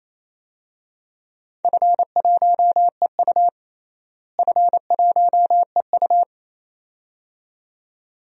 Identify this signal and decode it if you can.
Morse code, heard as a keyed tone, about 28 wpm, 700 Hz. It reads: F1EU F1EU
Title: DXCC-100 Call Signs - 4 Characters - Repeated 2x - 28wpm